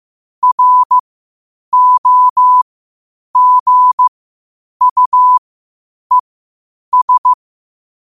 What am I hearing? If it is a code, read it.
Morse code, heard as a keyed tone, about 15 wpm, spaced as for 8 wpm, 1000 Hz. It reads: ROGUES
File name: qotdfiles / 8wpm